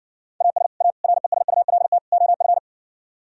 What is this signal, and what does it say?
Morse code, heard as a keyed tone, about 68 wpm, 700 Hz. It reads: AS A RESULT OF